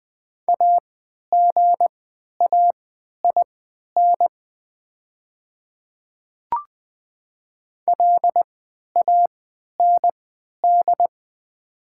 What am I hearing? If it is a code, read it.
Morse code, heard as a keyed tone, about 20 wpm, spaced as for 11 wpm, 700 Hz. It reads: AGAIN LAND
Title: 3x Character Spacing - Top 200 Words - 20wpm